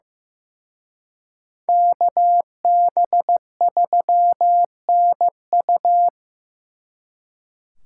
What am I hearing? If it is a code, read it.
Morse code, heard as a keyed tone, about 18 wpm, 700 Hz. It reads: KB3NU
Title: Speed Racing - USA 2x2 Call Signs 15wpm